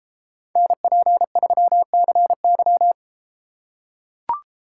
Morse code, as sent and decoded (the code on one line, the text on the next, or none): -. .--. ...-- -.-. -.--
NP3CY